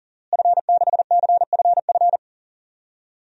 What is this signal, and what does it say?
Morse code, heard as a keyed tone, about 40 wpm, 700 Hz. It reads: F6CFF
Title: DXCC-100 Call Signs - 5 Characters - Repeated 3x - 40wpm